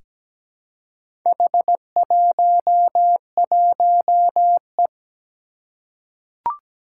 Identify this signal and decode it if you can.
Morse code, heard as a keyed tone, about 17 wpm, 700 Hz. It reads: H11E
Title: Four Challenging Letter-Number Combinations 17wpm